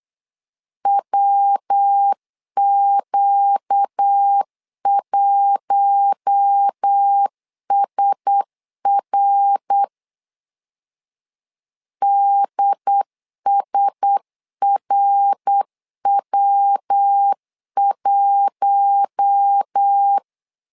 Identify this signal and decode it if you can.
Morse code, heard as a keyed tone, about 8 wpm, 800 Hz. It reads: WQ1SR DSRW1
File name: LEZIONI / 4 Q1S 8wpm 2710